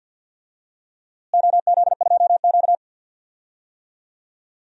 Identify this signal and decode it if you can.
Morse code, heard as a keyed tone, about 53 wpm, 700 Hz. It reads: OZ2Q
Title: DXCC-100 Call Signs - 4 Characters - Repeated 2x - 50wpm